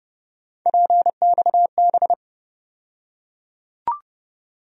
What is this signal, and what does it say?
Morse code, heard as a keyed tone, about 30 wpm, 700 Hz. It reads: PXB
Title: USA Call Sign Postfixes - 3 Letters 30wpm